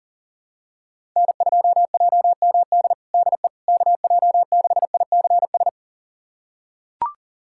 Morse code, heard as a keyed tone, about 37 wpm, 700 Hz. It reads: N2JMD DE KJ6ICS
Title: USA Callsign DE Callsign 40wpm